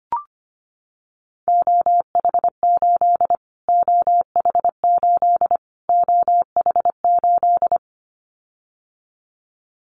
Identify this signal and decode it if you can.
Morse code, heard as a keyed tone, about 25 wpm, 700 Hz. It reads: OH8 OH8 OH8
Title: Callsign Prefixes for Top 25 DXCC - Repeated 3x - 25wpm